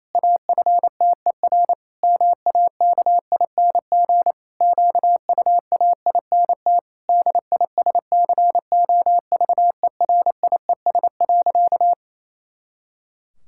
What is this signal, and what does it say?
Morse code, heard as a keyed tone, about 27 wpm, 700 Hz. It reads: AFTER MAKING QUAINT DISCOVERIES.
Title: Pangram Challenge v2 - 28wpm